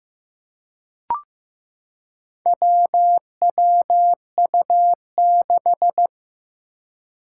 Morse code, heard as a keyed tone, about 15 wpm, 700 Hz. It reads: WWU6
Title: Four Challenging Letter-Number Combinations 15wpm